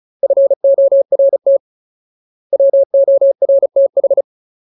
Morse code, heard as a keyed tone, about 34 wpm, 550 Hz. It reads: FORT WORTH
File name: Ultimate Word List / 35WPM Ultimate